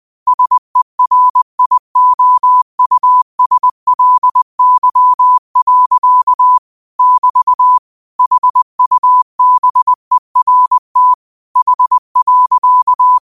Morse code, heard as a keyed tone, about 20 wpm, 1000 Hz. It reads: SERIOUSLY. <BT> HUBERT H.
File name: qotdfiles / 20wpm